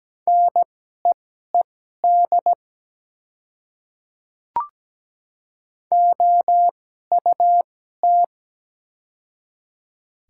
Morse code, heard as a keyed tone, about 17 wpm, spaced as for 12 wpm, 700 Hz. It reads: NEED OUT
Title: 2x Character Spacing - Top 200 Words - 17wpm